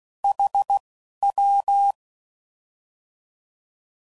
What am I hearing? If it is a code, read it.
Morse code, heard as a keyed tone, about 16 wpm, spaced as for 11 wpm, 800 Hz. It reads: HW
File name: Mixed / Misc / Words